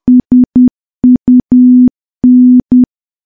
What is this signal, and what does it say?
Morse code, heard as a keyed tone, about 10 wpm, 250 Hz. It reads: SUN
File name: morse / morse-training-10WPM